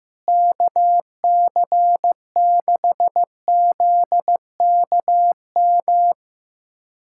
Morse code, heard as a keyed tone, about 15 wpm, 700 Hz. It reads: KC6ZKM
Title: USA Callsign DE Callsign - 6 Characters - 15wpm